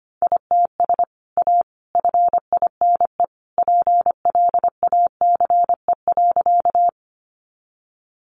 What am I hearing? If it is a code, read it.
Morse code, heard as a keyed tone, about 25 wpm, 700 Hz. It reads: ITS A FINE PLACE.
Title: Sentences from Top 600 Words 25wpm